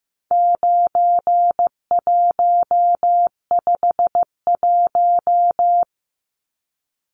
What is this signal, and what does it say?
Morse code, heard as a keyed tone, about 15 wpm, 700 Hz. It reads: 9151
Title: Numbers - 4-Digit 15wpm